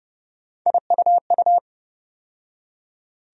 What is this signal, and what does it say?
Morse code, heard as a keyed tone, about 30 wpm, 700 Hz. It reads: IUU
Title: USA Call Sign Postfixes - 3 Letters 30wpm